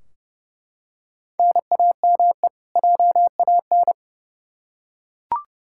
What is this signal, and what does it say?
Morse code, heard as a keyed tone, about 31 wpm, 700 Hz. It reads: NAME JAN
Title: Practice QSO - Exchange 1 - Beginning - 30wpm